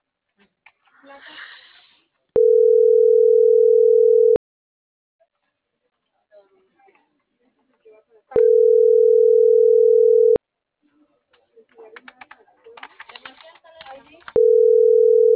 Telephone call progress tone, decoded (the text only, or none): ringback tone